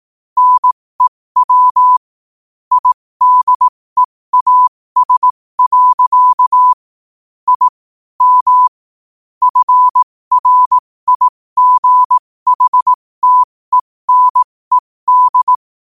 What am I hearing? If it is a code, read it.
Morse code, heard as a keyed tone, about 18 wpm, spaced as for 15 wpm, 1000 Hz. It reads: NEW IDEAS. I M FRIGHTENED
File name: qotdfiles / 15wpm